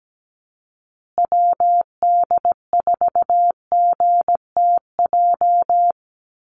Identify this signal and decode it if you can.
Morse code, heard as a keyed tone, about 17 wpm, 700 Hz. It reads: WD4GTJ
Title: USA Call Signs Repeated 2x 17wpm